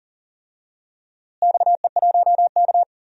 Morse code, heard as a keyed tone, about 40 wpm, 700 Hz. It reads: XE1K
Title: DXCC-100 Call Signs - 4 Characters - Repeated 2x - 40wpm